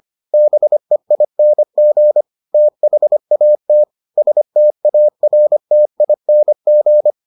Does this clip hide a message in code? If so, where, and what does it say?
Yes: Morse code from 0.3 to 7.1 s, heard as a keyed tone, about 25 wpm, 600 Hz. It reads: BEING THAT STARTING